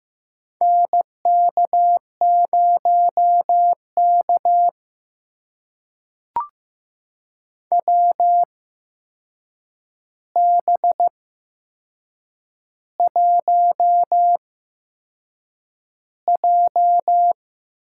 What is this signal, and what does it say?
Morse code, heard as a keyed tone, about 15 wpm, spaced as for 6 wpm, 700 Hz. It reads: NK0K WB1J
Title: USA Call Signs - 4 Characters - 8x Character Spacing - 15wpm